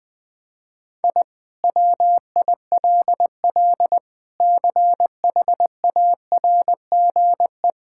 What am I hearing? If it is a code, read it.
Morse code, heard as a keyed tone, about 21 wpm, 700 Hz. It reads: I WILL CHARGE
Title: Sentences from Top 1000 Words - Review Part 6 - 20wpm